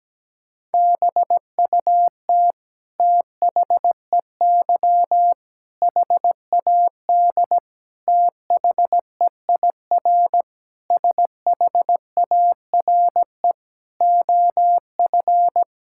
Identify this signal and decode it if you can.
Morse code, heard as a keyed tone, about 17 wpm, 700 Hz. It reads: BUT THEY HAD THEIR SHARE OF
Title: Sentences from Top 1000 Words - Review Part 6 - 17wpm